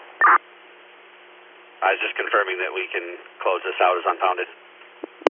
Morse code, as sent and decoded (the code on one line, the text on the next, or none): none